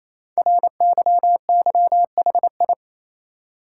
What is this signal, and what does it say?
Morse code, heard as a keyed tone, about 27 wpm, 700 Hz. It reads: RYYHI
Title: Code Groups - 5 Letters - 28wpm